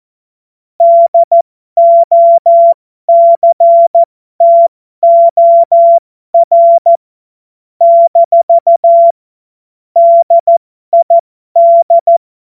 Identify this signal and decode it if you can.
Morse code, heard as a keyed tone, about 14 wpm, spaced as for 12 wpm, 680 Hz. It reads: DOCTOR - DID
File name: Morsekurs / Witze